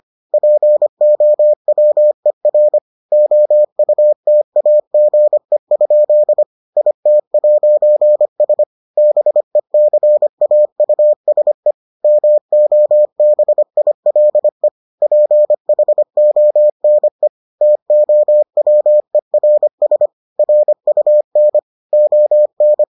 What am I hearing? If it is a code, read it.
Morse code, heard as a keyed tone, about 25 wpm, 600 Hz. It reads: POWER OUTAGE? IT'S BECAUSE MOBILE PHONE TOWERS RUN ON